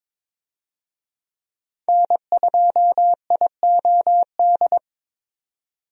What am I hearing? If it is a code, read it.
Morse code, heard as a keyed tone, about 22 wpm, 700 Hz. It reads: N2IOD
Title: USA Call Signs - 5 Characters - 22wpm